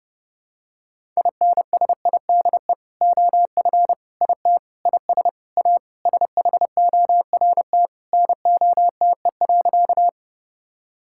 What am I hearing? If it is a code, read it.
Morse code, heard as a keyed tone, about 30 wpm, 700 Hz. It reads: INSIDE OF IT IS A SHORT NOTE.
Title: Sentences from Top 1000 Words - Review Part 6 - 30wpm